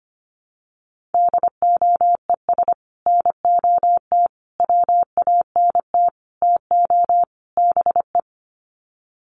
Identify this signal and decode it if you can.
Morse code, heard as a keyed tone, about 25 wpm, 700 Hz. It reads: DOES NOT WANT TO BE
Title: Sets of 5 Words Encore 25wpm